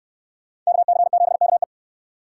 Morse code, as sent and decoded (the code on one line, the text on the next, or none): -.- -.-. --... -.- .
KC7KE